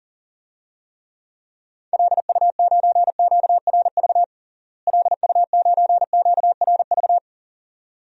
Morse code, heard as a keyed tone, about 40 wpm, 700 Hz. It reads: LU9QRV LU9QRV
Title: DXCC-100 Call Signs - 6 Characters - Repeated 2x - 40wpm